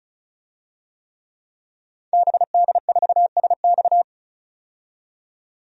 Morse code, heard as a keyed tone, about 36 wpm, 700 Hz. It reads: BD4SX
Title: DXCC-100 Call Signs - 5 Characters - 35wpm